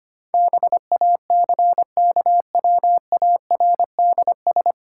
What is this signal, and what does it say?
Morse code, heard as a keyed tone, about 25 wpm, 700 Hz. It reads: BACKWARDS
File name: Nine Letter Words - 25wpm